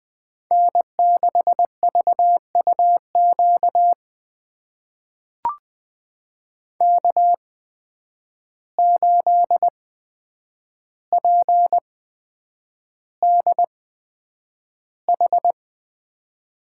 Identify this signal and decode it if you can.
Morse code, heard as a keyed tone, about 21 wpm, spaced as for 9 wpm, 700 Hz. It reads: N6VUQ K8PDH
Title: USA Call Signs - 5 Characters - 8x Character Spacing - 20wpm